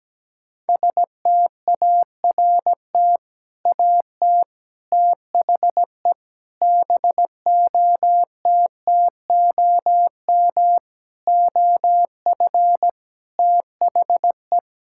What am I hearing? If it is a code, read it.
Morse code, heard as a keyed tone, about 17 wpm, 700 Hz. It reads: START AT THE BOTTOM OF THE
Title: Sentences from Top 1000 Words - Review Part 6 - 17wpm